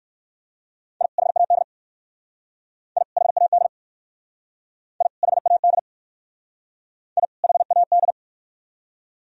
Morse code, heard as a keyed tone, about 57 wpm, 700 Hz. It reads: I HAD I HAD I HAD I HAD